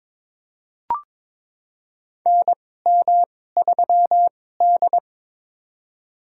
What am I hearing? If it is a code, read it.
Morse code, heard as a keyed tone, about 22 wpm, spaced as for 16 wpm, 700 Hz. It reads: NM3D
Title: USA Call Signs - 4 Characters - 2x Character Spacing - 22wpm